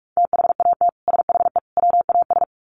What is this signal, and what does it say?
Morse code, heard as a keyed tone, about 44 wpm, 700 Hz. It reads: THAT SHE WAS
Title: Sets of 3 Words 45wpm